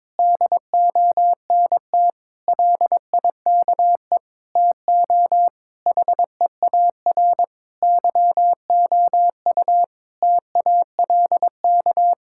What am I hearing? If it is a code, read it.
Morse code, heard as a keyed tone, about 22 wpm, 700 Hz. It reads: DONT LIKE TO HEAR YOU TALK
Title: Sentences from Top 1000 Words - Review Part 6 - 22wpm